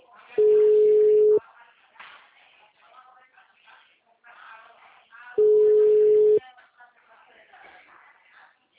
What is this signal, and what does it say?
Ringback tone (Europe, 425 Hz), with speech.
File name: out- -2011- - - .5719